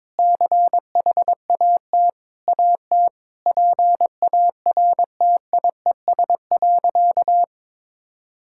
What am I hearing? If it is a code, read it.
Morse code, heard as a keyed tone, about 21 wpm, 700 Hz. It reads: CHAT AT PARTIES.
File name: practice / 22wpm / Easy Sentences with Hints - 22wpm